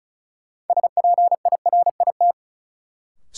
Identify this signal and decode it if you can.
Morse code, heard as a keyed tone, about 36 wpm, 700 Hz. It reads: SPIRIT
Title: Sentences from Top 1400 Words - New Word Drill - 35wpm